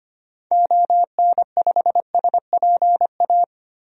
Morse code, heard as a keyed tone, about 25 wpm, 700 Hz. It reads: ON5SPA
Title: DXCC-100 Call Signs - 6 Characters - Repeated 2x - 25wpm